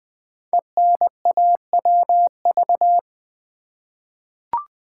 Morse code, heard as a keyed tone, about 20 wpm, 700 Hz. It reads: ENAWV
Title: Code Groups - 5 Letters - 20wpm